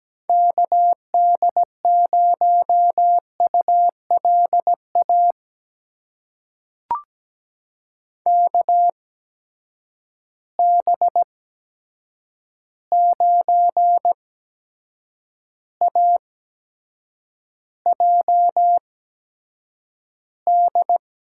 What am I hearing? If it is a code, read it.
Morse code, heard as a keyed tone, about 17 wpm, spaced as for 7 wpm, 700 Hz. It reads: KD0ULA KB9AJD
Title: USA Call Signs - 6 Characters - 8x Character Spacing - 17wpm